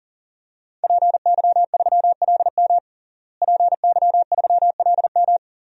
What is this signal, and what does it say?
Morse code, heard as a keyed tone, about 40 wpm, 700 Hz. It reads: PY3LM PY3LM
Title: DXCC-100 Call Signs - 5 Characters - Repeated 3x - 40wpm